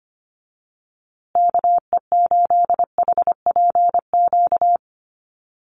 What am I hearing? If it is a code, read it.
Morse code, heard as a keyed tone, about 25 wpm, 700 Hz. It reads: KE8HPQ
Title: USA Call Signs Repeated 3x 25wpm